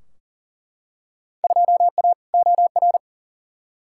Morse code, heard as a keyed tone, about 38 wpm, 700 Hz. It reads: 2A OR
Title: Field Day Exchanges Encore - 40wpm